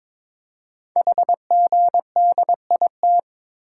Morse code, heard as a keyed tone, about 22 wpm, 700 Hz. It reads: HGDIT
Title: Code Groups - 5 Letters - 22wpm